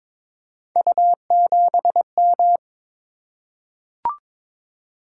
Morse code, heard as a keyed tone, about 22 wpm, 700 Hz. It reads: U7M
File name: practice / 22wpm / Three Letter-Number Combinations - 22wpm